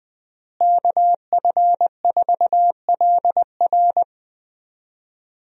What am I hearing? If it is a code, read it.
Morse code, heard as a keyed tone, about 21 wpm, 700 Hz. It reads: KF4LR
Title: USA Call Signs - 5 Characters - Repeated 3x - 20wpm